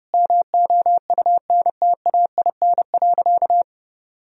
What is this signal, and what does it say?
Morse code, heard as a keyed tone, about 30 wpm, 700 Hz. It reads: MOUNTAIN.
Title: Sentences from Top 1000 Words - Review Part 1 - 30wpm